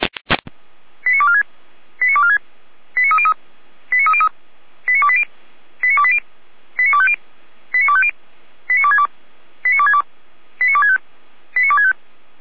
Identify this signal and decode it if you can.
Five-tone selective call (ZVEI), heard as quick runs of five tones, tones of about 80 ms. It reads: 89246 89246 89303 89303 89280 89280 89250 89250 89262 89262 89265 89265